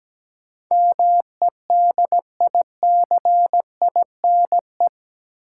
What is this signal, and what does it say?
Morse code, heard as a keyed tone, about 17 wpm, 700 Hz. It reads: MEDICINE
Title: Sentences from Top 900 Words - New Word Drill - 17wpm